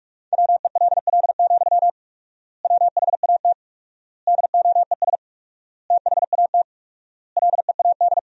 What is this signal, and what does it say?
Morse code, heard as a keyed tone, about 46 wpm, 700 Hz. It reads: WELL, WHAT DOES THAT LEAD
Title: 4x Word-Spacing - Sentences from Top 500 Words - 45wpm